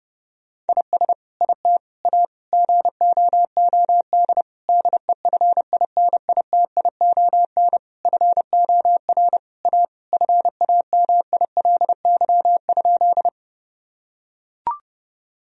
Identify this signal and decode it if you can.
Morse code, heard as a keyed tone, about 30 wpm, 700 Hz. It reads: IS IT A GOOD DEFINITION FOR A FAMILY?
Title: Sentences from Top 1000 Words - Review Part 4 - 30wpm